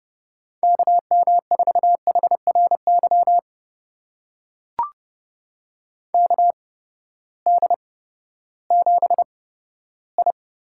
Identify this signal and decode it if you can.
Morse code, heard as a keyed tone, about 30 wpm, spaced as for 15 wpm, 700 Hz. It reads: KM4HRY KD7I
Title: USA Call Signs - 6 Characters - 8x Character Spacing - 30wpm